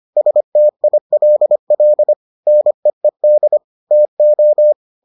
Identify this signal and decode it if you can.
Morse code, heard as a keyed tone, about 25 wpm, 600 Hz. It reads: STILL NEED TO